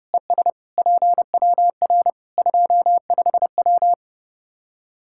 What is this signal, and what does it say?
Morse code, heard as a keyed tone, about 32 wpm, 700 Hz. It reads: ES PWR 25W
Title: Practice QSO - Exchange 2 - Info - 30wpm